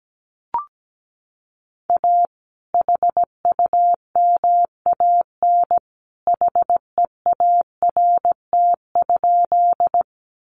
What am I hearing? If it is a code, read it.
Morse code, heard as a keyed tone, about 17 wpm, 700 Hz. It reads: A HUMAN HEART?
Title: Sentences from Top 1200 Words 17wpm